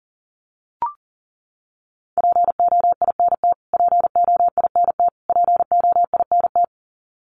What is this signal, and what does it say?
Morse code, heard as a keyed tone, about 40 wpm, 700 Hz. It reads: POINT POINT POINT
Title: Top 300 Words Repeated 40wpm